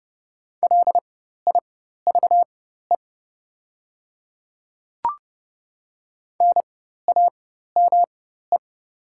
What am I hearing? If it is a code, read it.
Morse code, heard as a keyed tone, about 30 wpm, spaced as for 14 wpm, 700 Hz. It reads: LIVE NAME
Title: 4x Character Spacing - Top 200 Words - 30wpm